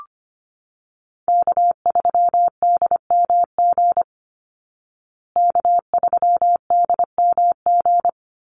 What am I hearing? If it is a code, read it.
Morse code, heard as a keyed tone, about 25 wpm, 700 Hz. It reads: K3DMG K3DMG
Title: Call Signs Repeated 2x - 25wpm